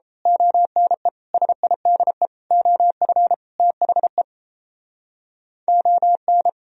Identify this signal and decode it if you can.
Morse code, heard as a keyed tone, about 32 wpm, 700 Hz. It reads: ONE SIDE OF THE ON
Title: Speed-Racing - Sets of 5 Words 28wpm